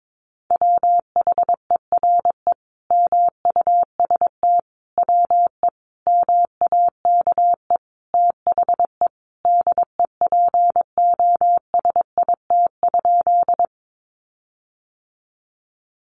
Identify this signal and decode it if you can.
Morse code, heard as a keyed tone, about 22 wpm, 700 Hz. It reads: WHERE MUST WE MAKE THE DEPOSIT?
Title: Sentences from Top 1300 Words 22wpm